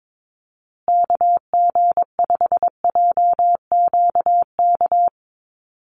Morse code, heard as a keyed tone, about 22 wpm, 700 Hz. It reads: KG5JQK
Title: USA Call Signs Repeated 2x 22wpm